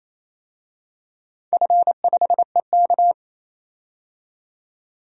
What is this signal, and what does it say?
Morse code, heard as a keyed tone, about 28 wpm, 700 Hz. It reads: F5EK
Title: DXCC-100 Call Signs - 4 Characters - 28wpm